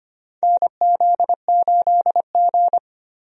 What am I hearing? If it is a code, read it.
Morse code, heard as a keyed tone, about 25 wpm, 700 Hz. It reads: NZ8G
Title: USA Call Signs - 4 Characters - Repeated 3x - 25wpm